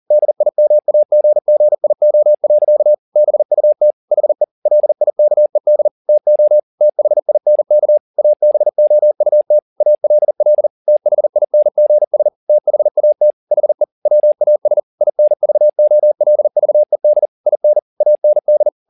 Morse code, heard as a keyed tone, about 40 wpm, 600 Hz. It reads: DIMAGGIO. BUT HE LIKED TO THINK ABOUT ALL THINGS THAT HE WAS INVOLVED IN AND
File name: the old man and the sea 2 40wpm